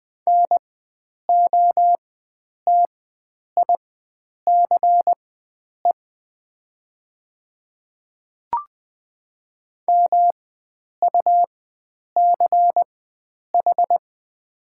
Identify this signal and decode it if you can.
Morse code, heard as a keyed tone, about 20 wpm, spaced as for 9 wpm, 700 Hz. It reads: NOTICE MUCH
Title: 4x Character Spacing - Top 500 Words - 20wpm